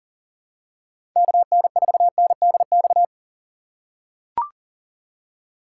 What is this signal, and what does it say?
Morse code, heard as a keyed tone, about 39 wpm, 700 Hz. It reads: KN4NDX
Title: USA Call Signs - 3x Character Spacing - 40wpm